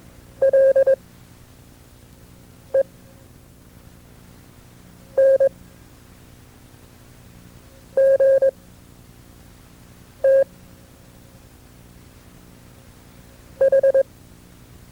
Morse code, heard as a keyed tone, about 21 wpm, spaced as for 4 wpm, 540 Hz. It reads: LENGTH
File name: FreeFiles / CW-practice / KB6MT-tape2-side2